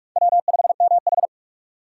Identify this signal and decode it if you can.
Morse code, heard as a keyed tone, about 46 wpm, 700 Hz. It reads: W5MH
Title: USA Call Signs - 4 Characters - 2x Character Spacing - 45wpm